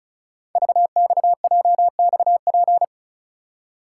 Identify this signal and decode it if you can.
Morse code, heard as a keyed tone, about 36 wpm, 700 Hz. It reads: VXJXP
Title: Code Groups - 5 Letter:Numbers - 35wpm